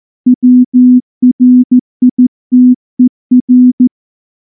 Morse code, heard as a keyed tone, about 14 wpm, 250 Hz. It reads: WRITER
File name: morse-training-15WPM